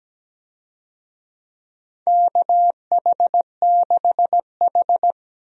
Morse code, heard as a keyed tone, about 17 wpm, 700 Hz. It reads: KH6H
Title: USA Call Signs - 4 Characters - 17wpm